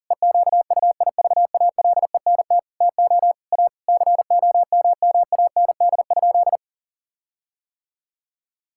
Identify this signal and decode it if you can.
Morse code, heard as a keyed tone, about 37 wpm, 700 Hz. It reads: EQUIVALENT TO A COMMAND?
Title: Sentences from Top 1600 Words 40wpm